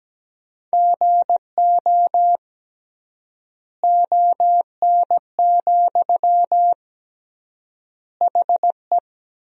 Morse code, heard as a keyed tone, about 17 wpm, 700 Hz. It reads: GO ON, HE
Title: 3x Word-Spacing - Sentences from Top 100 Words - 17wpm